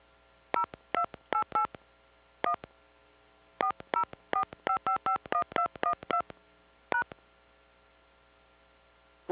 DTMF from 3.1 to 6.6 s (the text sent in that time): 4045551212